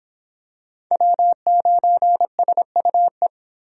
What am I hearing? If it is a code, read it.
Morse code, heard as a keyed tone, about 26 wpm, 700 Hz. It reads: W9SUE